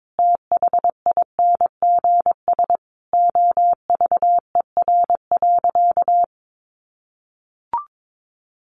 Morse code, heard as a keyed tone, about 22 wpm, 700 Hz. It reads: THINGS OVER.
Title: Sentences from Top 1400 Words 22wpm